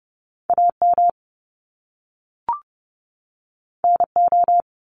Morse code, heard as a keyed tone, about 30 wpm, 700 Hz. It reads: AM NO
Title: Top Two Letter Combinations in Words 30wpm